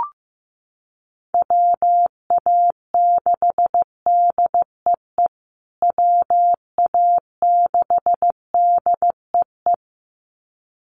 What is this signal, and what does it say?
Morse code, heard as a keyed tone, about 15 wpm, 700 Hz. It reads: WA6DEE WA6DEE
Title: USA Call Signs Repeated 2x 15wpm